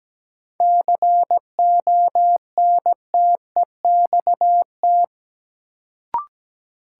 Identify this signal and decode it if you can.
Morse code, heard as a keyed tone, about 17 wpm, 700 Hz. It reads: CONTEXT
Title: Sets of 4 Words - New Word Drill - 17wpm